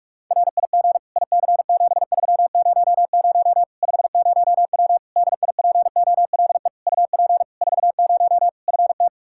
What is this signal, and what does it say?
Morse code, heard as a keyed tone, about 46 wpm, 700 Hz. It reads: RIG IC7300 50W DIPOLE UP 40 FT
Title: Practice QSO - Exchange 2 - Info - 45wpm